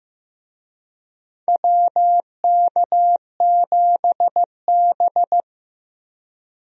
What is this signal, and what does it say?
Morse code, heard as a keyed tone, about 15 wpm, 700 Hz. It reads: WK7B
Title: USA Call Signs - 4 Characters - Repeated 3x - 15wpm